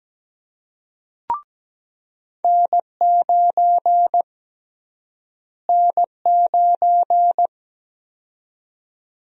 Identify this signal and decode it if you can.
Morse code, heard as a keyed tone, about 17 wpm, 700 Hz. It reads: N9 N9
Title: USA Call Sign Prefixes - Repeated 2x - 17wpm